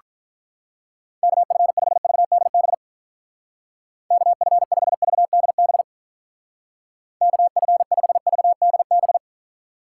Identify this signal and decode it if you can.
Morse code, heard as a keyed tone, about 46 wpm, 700 Hz. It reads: KF5VDB KF5VDB KF5VDB